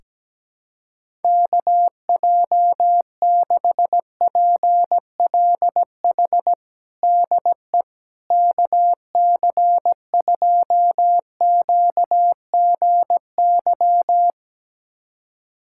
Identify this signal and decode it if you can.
Morse code, heard as a keyed tone, about 17 wpm, 700 Hz. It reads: KJ6PLH DE KC2QGY
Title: USA Callsign DE Callsign - 6 Characters - 17wpm